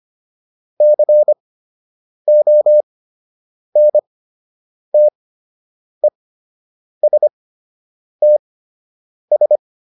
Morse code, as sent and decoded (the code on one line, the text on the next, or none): -.-. --- -. - . ... - ...
CONTESTS